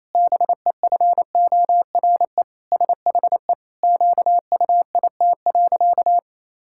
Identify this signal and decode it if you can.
Morse code, heard as a keyed tone, about 28 wpm, 700 Hz. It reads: BEFORE SHE QUIT.
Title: Pangram Challenge v2 - 28wpm